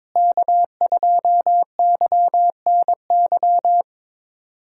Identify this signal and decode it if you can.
Morse code, heard as a keyed tone, about 22 wpm, 700 Hz. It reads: K2YNY
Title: USA Callsign DE Callsign - 5 Characters - 22wpm